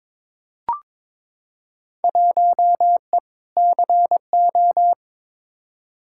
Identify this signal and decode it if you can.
Morse code, heard as a keyed tone, about 22 wpm, 700 Hz. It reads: 1E CO
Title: Field Day Exchanges - 22wpm